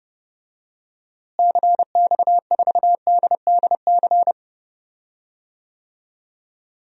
Morse code, heard as a keyed tone, about 30 wpm, 700 Hz. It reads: CX4DDC
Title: DXCC-100 Call Signs - 6 Characters - Repeated 3x - 30wpm